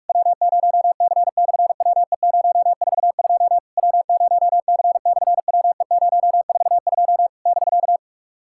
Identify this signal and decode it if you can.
Morse code, heard as a keyed tone, about 48 wpm, 700 Hz. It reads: W0C/WE042 W0C/WE042 <BK>
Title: SOTA QSO Practice 45wpm